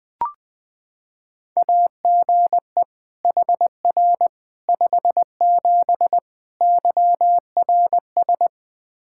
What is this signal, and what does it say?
Morse code, heard as a keyed tone, about 20 wpm, 700 Hz. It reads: AGE HR 57 YRS
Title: Practice QSO - Exchange 3 - More Info - 20wpm